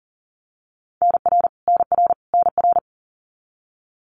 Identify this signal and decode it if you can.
Morse code, heard as a keyed tone, about 40 wpm, 700 Hz. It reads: NR NR NR
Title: Abbreviations Repeated 3x - 40wpm